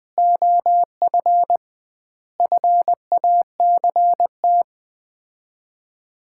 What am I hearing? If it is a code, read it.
Morse code, heard as a keyed tone, about 21 wpm, 700 Hz. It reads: OF FACT
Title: 2x Word-Spacing - Sets of 5 Words - 20wpm